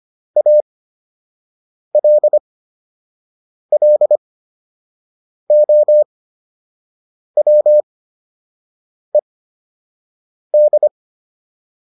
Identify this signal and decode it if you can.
Morse code, heard as a keyed tone, about 25 wpm, spaced as for 6 wpm, 600 Hz. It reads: ALLOWED
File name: ED8 06